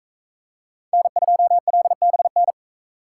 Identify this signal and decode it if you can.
Morse code, heard as a keyed tone, about 42 wpm, 700 Hz. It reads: N2LBN